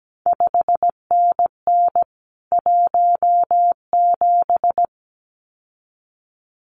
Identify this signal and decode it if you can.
Morse code, heard as a keyed tone, about 17 wpm, 700 Hz. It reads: 5NN 17
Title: CQ WW DX Contest Exchange Encore - 17wpm